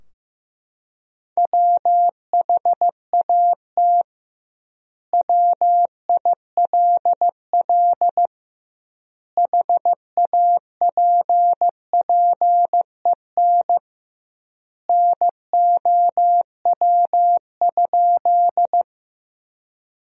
Morse code, heard as a keyed tone, about 15 wpm, 700 Hz. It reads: WHAT WILL HAPPEN NOW?
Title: 2x Word-Spacing - Sentences from Top 400 Words - 15wpm